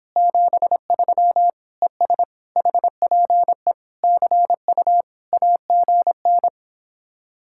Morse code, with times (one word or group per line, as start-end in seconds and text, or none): 0.2-1.5 73
1.8-2.2 ES
2.6-3.7 HPE
4.0-5.0 CU
5.3-6.5 AGN